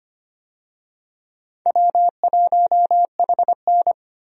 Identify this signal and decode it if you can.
Morse code, heard as a keyed tone, about 25 wpm, 700 Hz. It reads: W1HN
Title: USA Call Signs - 4 Characters - Repeated 3x - 25wpm